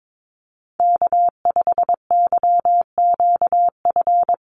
Morse code, heard as a keyed tone, about 22 wpm, 700 Hz. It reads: K5YQF
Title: Call Signs - 22wpm